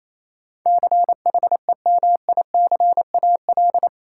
Morse code, heard as a keyed tone, about 28 wpm, 700 Hz. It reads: CHEMICAL
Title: Sentences from Top 1200 Words - New Word Drill - 28wpm